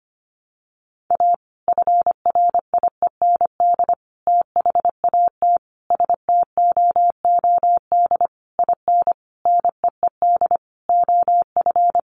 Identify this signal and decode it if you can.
Morse code, heard as a keyed tone, about 25 wpm, 700 Hz. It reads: A FRIEND THAT STOOD IN NEED OF
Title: Sentences from Top 1500 Words 25wpm